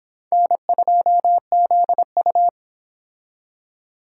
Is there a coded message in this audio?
Yes: Morse code from 0.3 to 2.5 s, heard as a keyed tone, about 25 wpm, 700 Hz. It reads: N2ZU